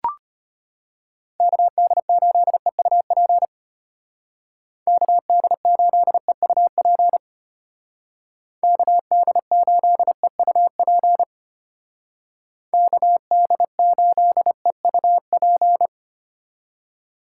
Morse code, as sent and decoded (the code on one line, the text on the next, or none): -.- -.. ---.. ..- .--. / -.- -.. ---.. ..- .--. / -.- -.. ---.. ..- .--. / -.- -.. ---.. ..- .--.
KD8EUP KD8EUP KD8EUP KD8EUP